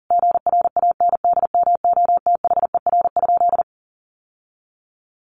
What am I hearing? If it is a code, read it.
Morse code, heard as a keyed tone, about 40 wpm, 700 Hz. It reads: GRANDMOTHER?
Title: Sentences from Top 1200 Words 40wpm